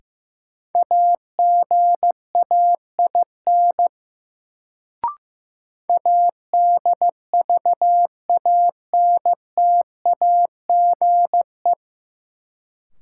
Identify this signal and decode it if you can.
Morse code, heard as a keyed tone, about 15 wpm, 700 Hz. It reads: AGAIN ADVANTAGE
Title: Sets of 4 Words - New Word Drill - 15wpm